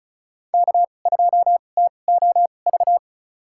Morse code, heard as a keyed tone, about 36 wpm, spaced as for 25 wpm, 700 Hz. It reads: K2TOV
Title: USA Call Signs - 5 Characters - 2x Character Spacing - 35wpm